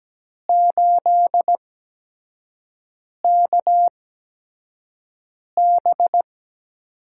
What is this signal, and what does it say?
Morse code, heard as a keyed tone, about 17 wpm, spaced as for 5 wpm, 700 Hz. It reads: 8KB
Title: USA Call Signs - 6 Characters - 8x Character Spacing - 17wpm